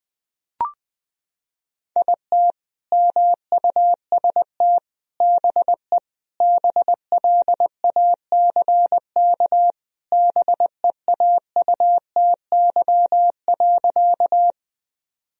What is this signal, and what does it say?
Morse code, heard as a keyed tone, about 20 wpm, 700 Hz. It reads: IT MUST BE BLACK BEAUTY.
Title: Sentences from Top 1000 Words - Review Part 2 - 20wpm